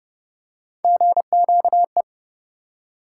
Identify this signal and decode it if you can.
Morse code, heard as a keyed tone, about 30 wpm, 700 Hz. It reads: GQE